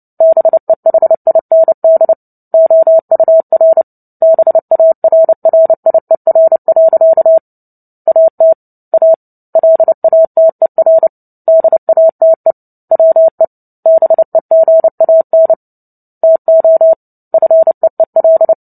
Morse code, heard as a keyed tone, about 30 wpm, 650 Hz. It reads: BEHIND OUR BARRIER. AT A LATER DATE WE BEGAN TO FEEL
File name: Book Two / 03-Chapter 3 - 29 WPM